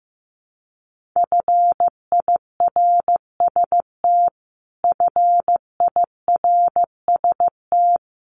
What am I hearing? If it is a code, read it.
Morse code, heard as a keyed tone, about 14 wpm, 700 Hz. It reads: FIRST FIRST
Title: Top 300 Words Repeated 15wpm